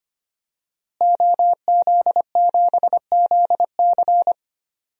Morse code, heard as a keyed tone, about 25 wpm, 700 Hz. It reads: OZ7ZC
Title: DXCC-100 Call Signs - 5 Characters - Repeated 2x - 25wpm